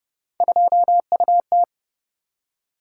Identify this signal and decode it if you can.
Morse code, heard as a keyed tone, about 32 wpm, 700 Hz. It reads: 2UT